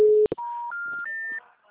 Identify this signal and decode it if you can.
Special information tone.
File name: out- -2023- - - .29828